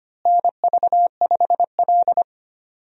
Morse code, heard as a keyed tone, about 25 wpm, 700 Hz. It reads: NV5L
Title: USA Call Signs - 4 Characters - Repeated 3x - 25wpm